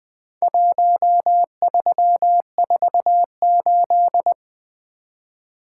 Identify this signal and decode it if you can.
Morse code, heard as a keyed tone, about 20 wpm, 700 Hz. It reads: 1348
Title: Four Letter-Number Combinations - 20wpm